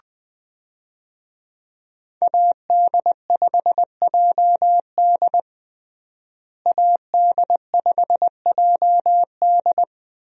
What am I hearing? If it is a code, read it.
Morse code, heard as a keyed tone, about 21 wpm, 700 Hz. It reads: AD5JD AD5JD
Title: USA Call Signs - 5 Characters - Repeated 2x - 20wpm